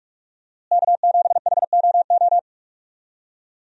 Morse code, heard as a keyed tone, about 46 wpm, 700 Hz. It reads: K7HOO
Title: USA Call Signs - 5 Characters - Repeated 3x - 45wpm